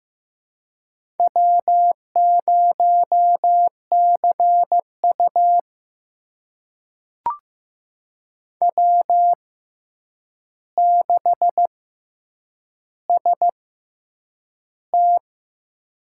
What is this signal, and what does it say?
Morse code, heard as a keyed tone, about 15 wpm, spaced as for 8 wpm, 700 Hz. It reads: W0CU W6ST
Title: USA Call Signs - 4 Characters - 6x Character Spacing - 15wpm